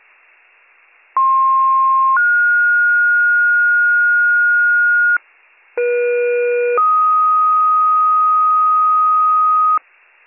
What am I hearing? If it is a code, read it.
Two-tone page, heard as a one-second tone, then a three-second tone, about 1060 then 1430 Hz, about 480 then 1250 Hz. It is heard from 1.2 to 9.8 s.